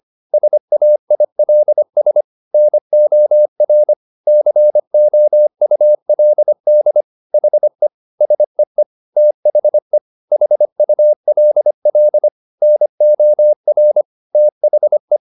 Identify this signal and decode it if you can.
Morse code, heard as a keyed tone, about 25 wpm, 600 Hz. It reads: SAILS NOR COULD HE SEE THE HULL NOR THE